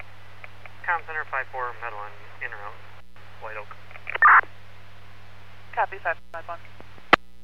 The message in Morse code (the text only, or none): none